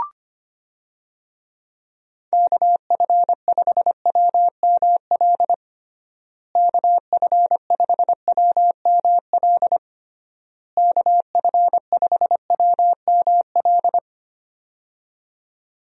Morse code, heard as a keyed tone, about 25 wpm, 700 Hz. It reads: KF5WML KF5WML KF5WML
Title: USA Call Signs - 6 Characters - Repeated 3x - 25wpm